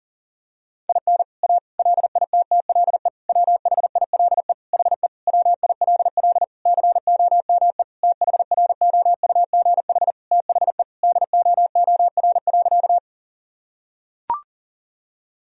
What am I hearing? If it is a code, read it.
Morse code, heard as a keyed tone, about 37 wpm, 700 Hz. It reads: IN A LITTLE WHILE HE WILL COME THROUGH THE DOOR.
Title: Sentences from Top 1000 Words - Review Part 1 - 40wpm